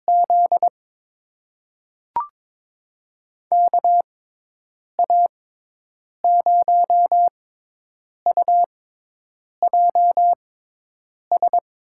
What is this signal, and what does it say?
Morse code, heard as a keyed tone, about 22 wpm, spaced as for 8 wpm, 700 Hz. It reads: Z KA0UJS
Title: USA Call Signs - 6 Characters - 6x Character Spacing - 22wpm